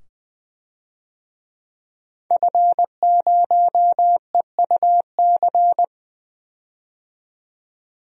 Morse code, heard as a keyed tone, about 20 wpm, 700 Hz. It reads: F0EUC
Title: DXCC-100 Call Signs - 5 Characters - 20wpm